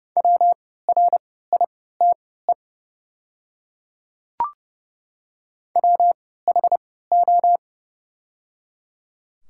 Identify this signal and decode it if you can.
Morse code, heard as a keyed tone, about 30 wpm, spaced as for 17 wpm, 700 Hz. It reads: WRITE WHO
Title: 3x Character Spacing - Top 100 Words - 30wpm